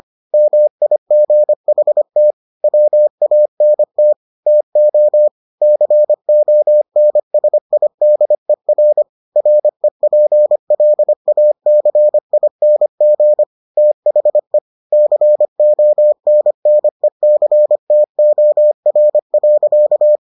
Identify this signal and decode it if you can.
Morse code, heard as a keyed tone, about 25 wpm, 600 Hz. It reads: MIGHT WANT TO CONSIDER REPLACING THE CONNECTOR.